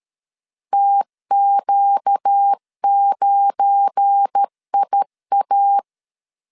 Morse code, heard as a keyed tone, about 13 wpm, 800 Hz. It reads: TQ9IA
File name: LEZIONI / 8 YJ45 / 8 YJ 12wpm 2235